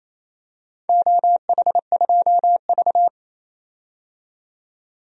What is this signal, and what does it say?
Morse code, heard as a keyed tone, about 28 wpm, 700 Hz. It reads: OH2V
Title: DXCC-100 Call Signs - 4 Characters - 28wpm